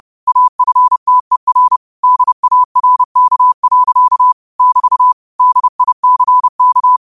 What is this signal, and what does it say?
Morse code, heard as a keyed tone, about 30 wpm, 1000 Hz. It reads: AFTER DARK. <BT> DICK
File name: qotdfiles / 30wpm